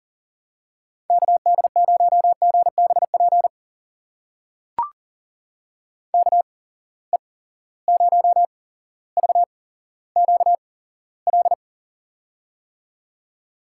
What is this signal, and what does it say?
Morse code, heard as a keyed tone, about 37 wpm, spaced as for 17 wpm, 700 Hz. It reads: KD0GBP KE0VQL
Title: USA Call Signs - 6 Characters - 8x Character Spacing - 40wpm